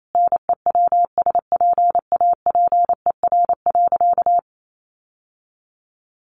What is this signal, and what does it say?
Morse code, heard as a keyed tone, about 28 wpm, 700 Hz. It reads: NEWSPAPER.
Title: Sentences from Top 1400 Words 28wpm